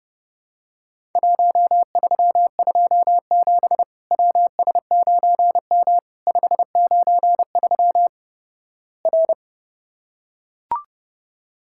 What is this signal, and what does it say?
Morse code, heard as a keyed tone, about 32 wpm, 700 Hz. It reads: 1327 WS9M 593 R
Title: Worked All Europe DX Contest - QTC Exchange - 30wpm